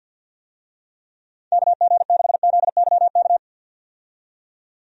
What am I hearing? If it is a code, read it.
Morse code, heard as a keyed tone, about 50 wpm, 700 Hz. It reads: KG6ZYK
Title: USA Call Signs - 6 Characters - 50wpm